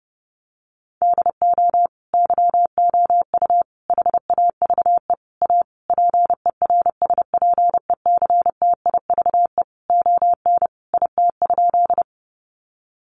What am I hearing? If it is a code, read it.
Morse code, heard as a keyed tone, about 30 wpm, 700 Hz. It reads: DO YOU HAVE A PERSPECTIVE ON IT?
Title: Sentences from Top 1200 Words 30wpm